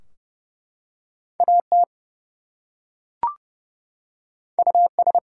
Morse code, heard as a keyed tone, about 30 wpm, 700 Hz. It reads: AT US
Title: Two Letter Words - 30wpm